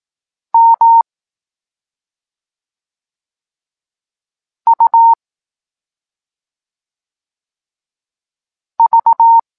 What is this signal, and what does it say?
Morse code, heard as a keyed tone, about 18 wpm, 920 Hz. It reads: M U V
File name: Tekst-9-rownomierny